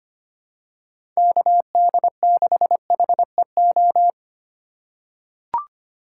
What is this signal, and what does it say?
Morse code, heard as a keyed tone, about 25 wpm, 700 Hz. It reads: KD6HEO
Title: USA Call Signs - 3x Character Spacing - 25wpm